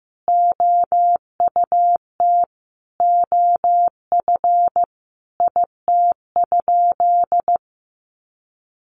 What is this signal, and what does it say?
Morse code, heard as a keyed tone, about 15 wpm, 700 Hz. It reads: OUT OF IT?
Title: Sentences from Top 300 Words 15wpm